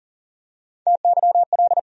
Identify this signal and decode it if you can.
Morse code, heard as a keyed tone, about 40 wpm, 700 Hz. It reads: TYL
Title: USA Call Sign Postfixes - 3 Letters 40wpm